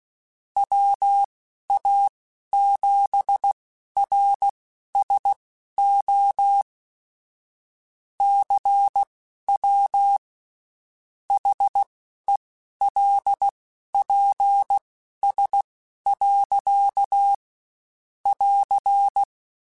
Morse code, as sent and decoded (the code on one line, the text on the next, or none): .-- .- --... .-. ... --- / -.-. .-- / .... .-.. .--. ... .-.-.- / .-.-.
WA7RSO CW HELPS. <AR>